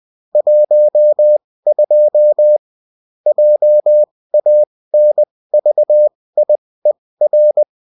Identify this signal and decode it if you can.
Morse code, heard as a keyed tone, about 20 wpm, spaced as for 16 wpm, 600 Hz. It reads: 12 JANVIER
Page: bulletin-uft-sous-forme-texte-et-en-cw